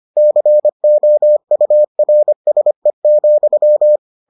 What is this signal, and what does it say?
Morse code, heard as a keyed tone, about 25 wpm, 600 Hz. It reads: COURSE,